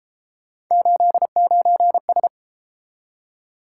Morse code, heard as a keyed tone, about 33 wpm, 700 Hz. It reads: 89S